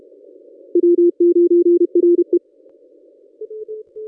Morse code, heard as a keyed tone, about 31 wpm, 350 Hz. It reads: W9RE
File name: contests / ss-cw-k5tr-op-n5rz -1100